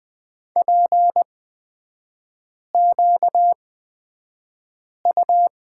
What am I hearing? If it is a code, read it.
Morse code, heard as a keyed tone, about 20 wpm, spaced as for 5 wpm, 700 Hz. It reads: PQU
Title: 250 Lesson 40e - BK - Three Characters 20wpm